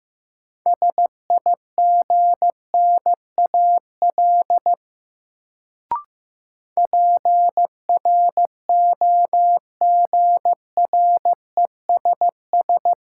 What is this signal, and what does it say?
Morse code, heard as a keyed tone, about 16 wpm, 700 Hz. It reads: SIGNAL PROGRESS
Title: Sentences from Top 1400 Words - New Word Drill Alt - 15wpm